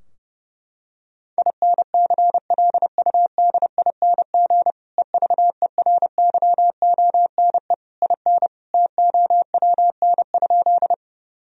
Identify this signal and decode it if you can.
Morse code, heard as a keyed tone, about 30 wpm, 700 Hz. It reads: INCLUDING EVERYONE IN TOWN?
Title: Sentences from Top 1000 Words - Review Part 4 - 30wpm